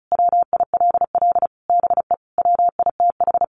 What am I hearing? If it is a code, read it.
Morse code, heard as a keyed tone, about 34 wpm, 700 Hz. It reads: WILL BE WITH
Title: Sentences from Top 600 Words 35wpm